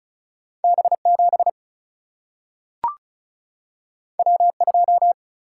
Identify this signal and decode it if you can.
Morse code, heard as a keyed tone, about 36 wpm, 700 Hz. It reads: B 7 W 2
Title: Single-Letter Single-Number 35wpm